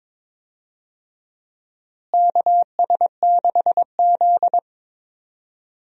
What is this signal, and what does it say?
Morse code, heard as a keyed tone, about 22 wpm, 700 Hz. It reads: KS6Z
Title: USA Call Signs - 4 Characters - Repeated 2x - 22wpm